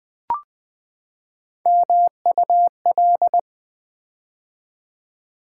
Morse code, heard as a keyed tone, about 20 wpm, 700 Hz. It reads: MUL
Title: Build A Word - 3 Letters At A Time - 8-Letter Words - 20wpm